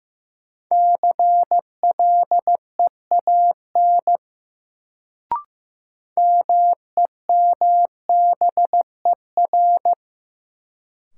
Morse code, heard as a keyed tone, about 15 wpm, 700 Hz. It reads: CLEAN MEMBER
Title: Sentences from Top 1100 Words - New Word Drill - 15wpm